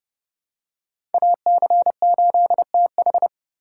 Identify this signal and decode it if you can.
Morse code, heard as a keyed tone, about 30 wpm, 700 Hz. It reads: AC8TH
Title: USA Call Signs - 5 Characters - 2x Character Spacing - 30wpm